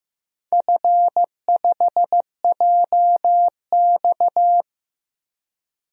Morse code, heard as a keyed tone, about 15 wpm, 700 Hz. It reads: F5JX
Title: DXCC-100 Call Signs - 4 Characters - 15wpm